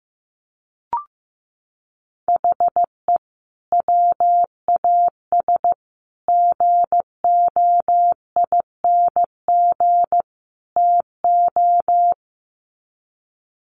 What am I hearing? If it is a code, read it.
Morse code, heard as a keyed tone, about 15 wpm, 700 Hz. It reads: HE WAS GOING TO
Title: Sets of 4 Words 15wpm